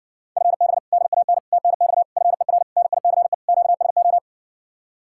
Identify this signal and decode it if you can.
Morse code, heard as a keyed tone, about 64 wpm, 700 Hz. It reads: FB DAN TNX FER NICE QSO